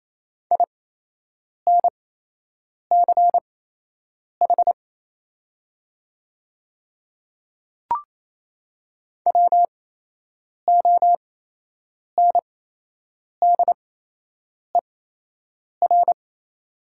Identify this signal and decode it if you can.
Morse code, heard as a keyed tone, about 28 wpm, spaced as for 8 wpm, 700 Hz. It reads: INCH WONDER
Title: 8x Character Spacing - Top 500 Words - 28wpm